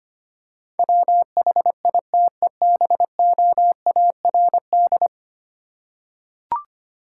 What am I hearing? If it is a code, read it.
Morse code, heard as a keyed tone, about 25 wpm, 700 Hz. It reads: WHITEBOARD
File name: practice / 25wpm / Easy Words with Hints - 25wpm